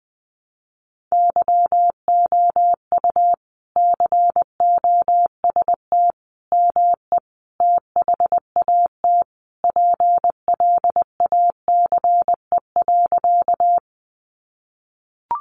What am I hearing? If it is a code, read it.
Morse code, heard as a keyed tone, about 20 wpm, 700 Hz. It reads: YOU COST ME THAT PLACE.
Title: Sentences from Top 800 Words 20wpm